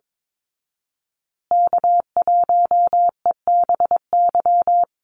Morse code, heard as a keyed tone, about 22 wpm, 700 Hz. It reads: K1EBY
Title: Call Signs Repeated 2x - 22wpm